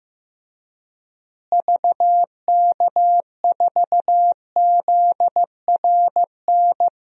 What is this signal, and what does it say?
Morse code, heard as a keyed tone, about 15 wpm, 700 Hz. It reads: VK4ZRN
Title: DXCC-100 Call Signs - 6 Characters - Repeated 2x - 15wpm